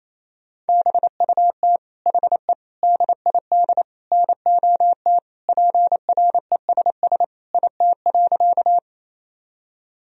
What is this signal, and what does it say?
Morse code, heard as a keyed tone, about 28 wpm, 700 Hz. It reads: BUT HE DID NOT PRESS IT.
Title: Sentences from Top 1000 Words - Review Part 1 - 28wpm